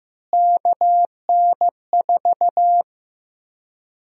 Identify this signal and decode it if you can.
Morse code, heard as a keyed tone, about 16 wpm, 700 Hz. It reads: KN4
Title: USA Call Sign Prefixes - Repeated 2x - 15wpm